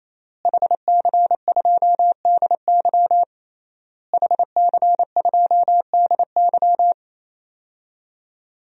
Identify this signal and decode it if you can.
Morse code, heard as a keyed tone, about 27 wpm, 700 Hz. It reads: HC2DY HC2DY
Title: DXCC-100 Call Signs - 5 Characters - Repeated 3x - 28wpm